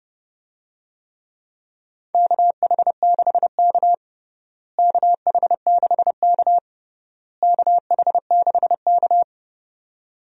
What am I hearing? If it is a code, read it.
Morse code, heard as a keyed tone, about 30 wpm, 700 Hz. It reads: KH6K KH6K KH6K
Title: DXCC-100 Call Signs - 4 Characters - Repeated 3x - 30wpm